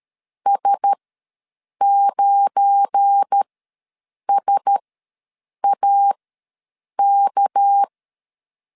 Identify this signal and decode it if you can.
Morse code, heard as a keyed tone, about 13 wpm, spaced as for 7 wpm, 800 Hz. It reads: S9SAK
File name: LEZIONI / 7 GF9Z / 7 GF9 ws 12wpm 3344